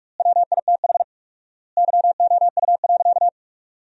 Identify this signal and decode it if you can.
Morse code, heard as a keyed tone, about 44 wpm, 700 Hz. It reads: WITH YOU.
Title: 4x Word-Spacing - Sentences from Top 300 Words - 45wpm